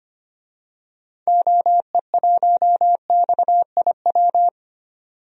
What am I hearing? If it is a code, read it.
Morse code, heard as a keyed tone, about 25 wpm, 700 Hz. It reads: OE1XIW
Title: DXCC-100 Call Signs - 6 Characters - 25wpm